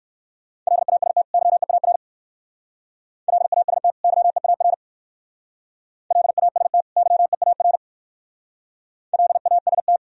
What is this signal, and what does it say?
Morse code, heard as a keyed tone, about 56 wpm, 700 Hz. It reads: LAST YEAR LAST YEAR LAST YEAR LAST
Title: Speed-Racing - Sets of 2 Words 45wpm